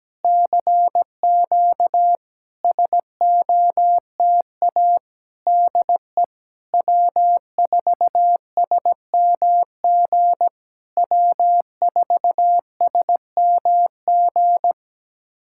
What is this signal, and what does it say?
Morse code, heard as a keyed tone, about 17 wpm, 700 Hz. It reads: CQ SOTA DE W4SMG W4SMG
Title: SOTA QSO Practice 17wpm